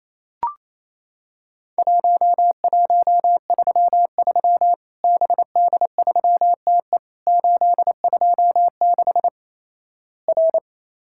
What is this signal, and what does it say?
Morse code, heard as a keyed tone, about 28 wpm, 700 Hz. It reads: 1133 BD3TE 826 R
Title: Worked All Europe DX Contest - QTC Exchange - 28wpm